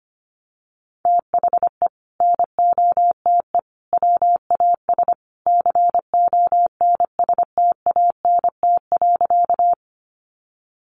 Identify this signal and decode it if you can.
Morse code, heard as a keyed tone, about 25 wpm, 700 Hz. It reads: THE NOTE WAS CONSTANT.
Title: Sentences from Top 1500 Words 25wpm